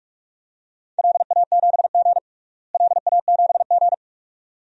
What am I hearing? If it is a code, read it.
Morse code, heard as a keyed tone, about 46 wpm, 700 Hz. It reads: LA7G LA7G
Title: DXCC-100 Call Signs - 4 Characters - Repeated 2x - 45wpm